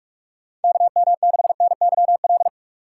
Morse code, heard as a keyed tone, about 46 wpm, 700 Hz. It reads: KM6NYL
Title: USA Call Signs - 6 Characters - 6x Character Spacing - 45wpm